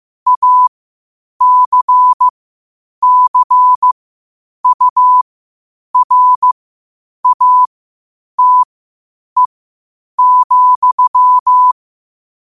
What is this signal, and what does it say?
Morse code, heard as a keyed tone, about 15 wpm, spaced as for 8 wpm, 1000 Hz. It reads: ACCURATE,
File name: qotdfiles / 8wpm